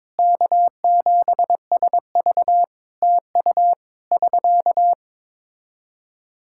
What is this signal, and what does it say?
Morse code, heard as a keyed tone, about 22 wpm, 700 Hz. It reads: K7SV TU <SK>
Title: K1USN SST QSO Practice 22wpm